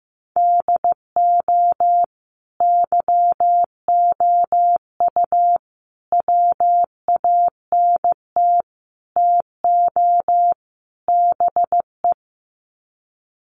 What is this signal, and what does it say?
Morse code, heard as a keyed tone, about 15 wpm, 700 Hz. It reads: DO YOU WANT TO BE
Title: Sets of 5 Words Encore2 15wpm